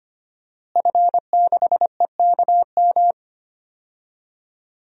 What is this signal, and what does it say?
Morse code, heard as a keyed tone, about 25 wpm, 700 Hz. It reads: F6EKM
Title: DXCC-100 Call Signs - 5 Characters - Repeated 2x - 25wpm